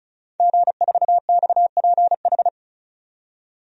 Morse code, heard as a keyed tone, about 36 wpm, 700 Hz. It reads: G4XPH
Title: DXCC-100 Call Signs - 5 Characters - 35wpm